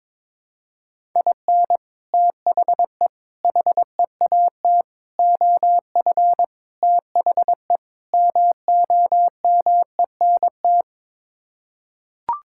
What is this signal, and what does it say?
Morse code, heard as a keyed tone, about 22 wpm, 700 Hz. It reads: IN THE HEAT OF THE MOMENT
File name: practice / 22wpm / Sets of 6 Words - 22wpm